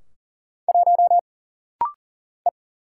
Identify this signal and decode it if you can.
Morse code, heard as a keyed tone, about 40 wpm, 700 Hz. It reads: 1 E
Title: Single Letter-Number - Rapid-Fire 40wpm